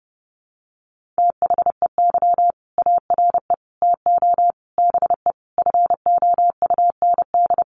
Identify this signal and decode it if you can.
Morse code, heard as a keyed tone, about 30 wpm, 700 Hz. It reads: THEY ARE TO BE FOUND
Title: Sentences from Top 1400 Words 30wpm